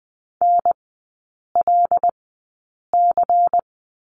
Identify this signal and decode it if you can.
Morse code, heard as a keyed tone, about 20 wpm, spaced as for 8 wpm, 700 Hz. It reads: NLC
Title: 100 Lesson 17e - Letter u - Three Characters 20wpm